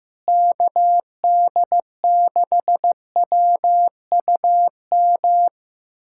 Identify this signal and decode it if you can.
Morse code, heard as a keyed tone, about 15 wpm, 700 Hz. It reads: KD6WUM
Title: USA Callsign DE Callsign - 6 Characters - 15wpm